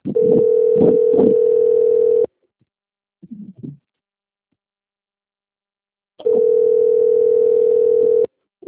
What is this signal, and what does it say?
Ringback tone (North America), 440 + 480 Hz.
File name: out- - - - - .30461